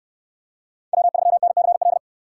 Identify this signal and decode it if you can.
Morse code, heard as a keyed tone, about 69 wpm, 700 Hz. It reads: W3NCL